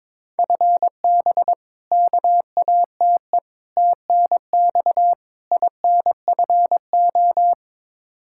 Morse code, heard as a keyed tone, about 22 wpm, 700 Hz. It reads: FB KATE TNX INFO
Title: Practice QSO - Exchange 3 - More Info - 22wpm